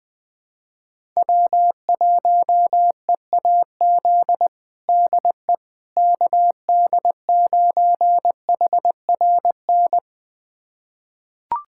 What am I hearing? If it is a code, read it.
Morse code, heard as a keyed tone, about 20 wpm, 700 Hz. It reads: W1EAZ DE KD9HRN
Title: USA Callsign DE Callsign 20wpm